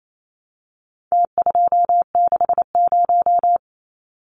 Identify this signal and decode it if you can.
Morse code, heard as a keyed tone, about 28 wpm, 700 Hz. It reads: T260
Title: Numbers - Contest Style - 4-Digit 28wpm